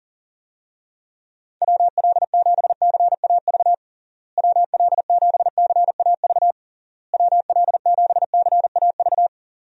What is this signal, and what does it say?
Morse code, heard as a keyed tone, about 40 wpm, 700 Hz. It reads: WL7CAV WL7CAV WL7CAV
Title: USA Call Signs - 6 Characters - Repeated 3x - 40wpm